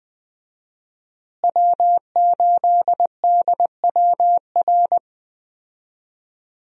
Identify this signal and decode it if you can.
Morse code, heard as a keyed tone, about 20 wpm, 700 Hz. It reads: W8DWR
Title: USA Call Signs - 5 Characters - Repeated 3x - 20wpm